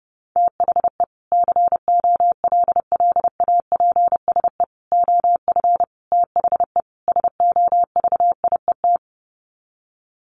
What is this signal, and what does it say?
Morse code, heard as a keyed tone, about 28 wpm, 700 Hz. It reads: THE COLLAPSE OF THE SOVIET
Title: Sets of 5 Words Encore 30wpm